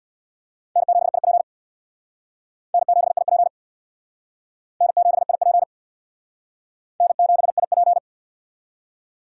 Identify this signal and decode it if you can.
Morse code, heard as a keyed tone, about 63 wpm, 700 Hz. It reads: N7IP N7IP N7IP N7IP